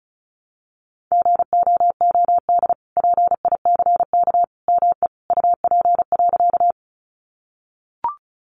Morse code, heard as a keyed tone, about 34 wpm, 700 Hz. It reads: GOOD PICK ME UP.
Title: Sentences from Top 1000 Words 35wpm